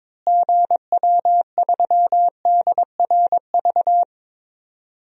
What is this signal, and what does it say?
Morse code, heard as a keyed tone, about 22 wpm, 700 Hz. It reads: GW3DRV
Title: DXCC-100 Call Signs - 6 Characters - Repeated 3x - 22wpm